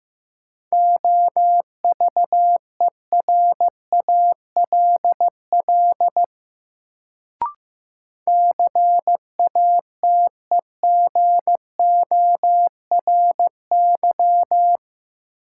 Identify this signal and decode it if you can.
Morse code, heard as a keyed tone, about 16 wpm, 700 Hz. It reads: OVERALL CATEGORY
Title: Sentences from Top 1000 Words - Review Part 7 - New Word Drill - 15wpm